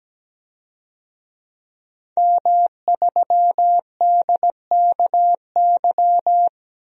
Morse code, heard as a keyed tone, about 17 wpm, 700 Hz. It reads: M3DKY
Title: DXCC-100 Call Signs - 5 Characters - 17wpm